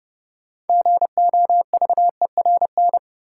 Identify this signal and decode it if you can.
Morse code, heard as a keyed tone, about 32 wpm, 700 Hz. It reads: GOVERN
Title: Sentences from Top 1000 Words - Review Part 2 - New Word Drill - 30wpm